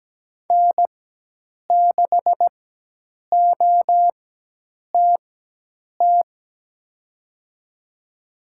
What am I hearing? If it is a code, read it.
Morse code, heard as a keyed tone, about 17 wpm, spaced as for 8 wpm, 700 Hz. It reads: N6OTT
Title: USA Call Signs - 5 Characters - 4x Character Spacing - 17wpm